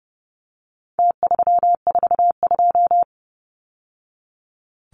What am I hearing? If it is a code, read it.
Morse code, heard as a keyed tone, about 30 wpm, 700 Hz. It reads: T342
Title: Numbers - Contest Style - 4-Digit 30wpm